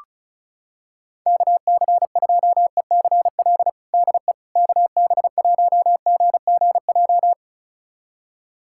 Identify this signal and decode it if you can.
Morse code, heard as a keyed tone, about 36 wpm, 700 Hz. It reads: KC2ECL DE KB1GGJ
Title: USA Callsign DE Callsign - 6 Characters - 35wpm